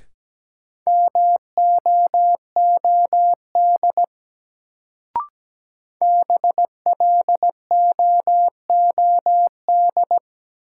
Morse code, heard as a keyed tone, about 17 wpm, 700 Hz. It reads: MOOD BLOOD
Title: Sentences from Top 1100 Words - New Word Drill Alt - 17wpm